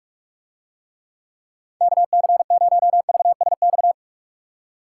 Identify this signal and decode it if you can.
Morse code, heard as a keyed tone, about 44 wpm, 700 Hz. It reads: KC0VSX